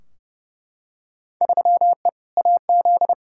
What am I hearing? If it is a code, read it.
Morse code, heard as a keyed tone, about 31 wpm, 700 Hz. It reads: 3E AZ
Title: Field Day Exchanges - 30wpm